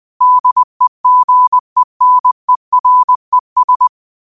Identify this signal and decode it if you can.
Morse code, heard as a keyed tone, about 20 wpm, 1000 Hz. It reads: DEGENERES
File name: 20wpm